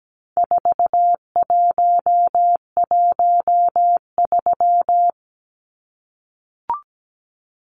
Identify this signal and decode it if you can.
Morse code, heard as a keyed tone, about 17 wpm, 700 Hz. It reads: 4113
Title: Numbers - 4-Digit 17wpm